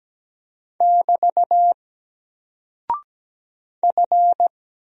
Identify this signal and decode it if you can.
Morse code, heard as a keyed tone, about 17 wpm, 700 Hz. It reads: <BT> F
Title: Single Character (Letter:Number:Punctuation:Prosign) Alt 17wpm